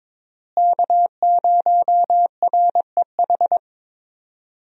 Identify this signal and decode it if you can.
Morse code, heard as a keyed tone, about 21 wpm, 700 Hz. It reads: K0REH
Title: USA Call Signs - 5 Characters - Repeated 2x - 22wpm